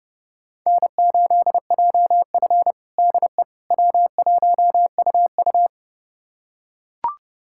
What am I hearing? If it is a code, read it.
Morse code, heard as a keyed tone, about 32 wpm, 700 Hz. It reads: N8JF DE W1UU
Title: USA Callsign DE Callsign - 4 Characters - 30wpm